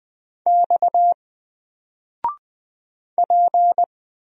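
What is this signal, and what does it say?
Morse code, heard as a keyed tone, about 20 wpm, 700 Hz. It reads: X P
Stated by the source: Single letter and number Morse drill at 20 wpm, from N4LCD.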